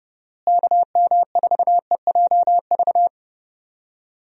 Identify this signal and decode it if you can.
Morse code, heard as a keyed tone, about 32 wpm, 700 Hz. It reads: KM4EJV
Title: USA Call Signs - 6 Characters - Repeated 2x - 30wpm